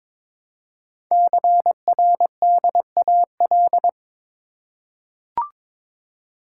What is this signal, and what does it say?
Morse code, heard as a keyed tone, about 22 wpm, 700 Hz. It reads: CRDAL
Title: Code Groups - 5 Letters - 22wpm